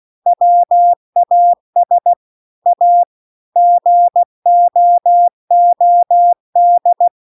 Morse code, heard as a keyed tone, about 16 wpm, 700 Hz. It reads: WAS A GOOD